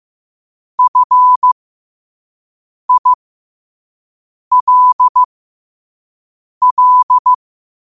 Morse code, heard as a keyed tone, about 15 wpm, spaced as for 5 wpm, 1000 Hz. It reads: FILL
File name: qotdfiles / 5wpm